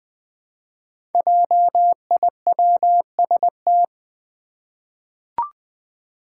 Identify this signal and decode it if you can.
Morse code, heard as a keyed tone, about 20 wpm, 700 Hz. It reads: JIWST
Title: Five Challenging Letter-Number Combinations 20wpm